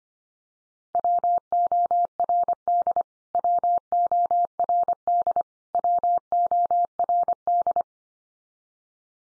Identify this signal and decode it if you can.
Morse code, heard as a keyed tone, about 25 wpm, 700 Hz. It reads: WORD WORD WORD
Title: Top 100 Words 25wpm x3